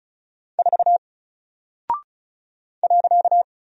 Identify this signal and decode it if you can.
Morse code, heard as a keyed tone, about 36 wpm, 700 Hz. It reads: 4 .
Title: Single Character (Letter:Number:Punctuation:Prosign) Alt 35wpm